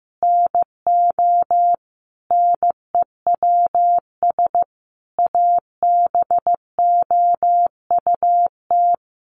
Morse code, heard as a keyed tone, about 14 wpm, 700 Hz. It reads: NO NEWS ABOUT
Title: Sentences from Top 1500 Words 15wpm